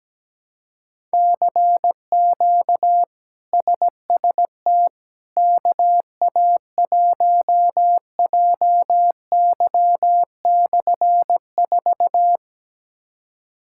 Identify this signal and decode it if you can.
Morse code, heard as a keyed tone, about 17 wpm, 700 Hz. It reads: CQ SST KA1JY/4
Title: K1USN SST QSO Practice - 17wpm